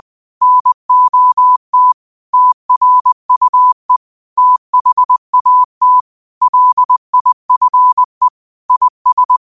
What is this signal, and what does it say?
Morse code, heard as a keyed tone, about 20 wpm, 1000 Hz. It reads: NOT TRUE THAT LIFE IS
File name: qotdfiles / 20wpm